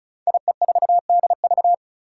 Morse code, heard as a keyed tone, about 36 wpm, 700 Hz. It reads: IE4DV
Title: Five Challenging Letter-Number Combinations 35wpm